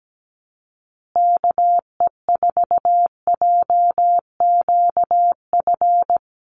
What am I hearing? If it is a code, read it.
Morse code, heard as a keyed tone, about 17 wpm, 700 Hz. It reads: KE4JQF
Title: USA Call Signs Repeated 3x 17wpm